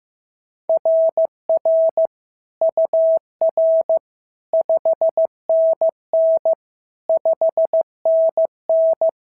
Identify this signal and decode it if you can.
Morse code, heard as a keyed tone, about 14 wpm, 650 Hz. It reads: RR UR 5NN 5NN
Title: SOTA QSO Practice 15wpm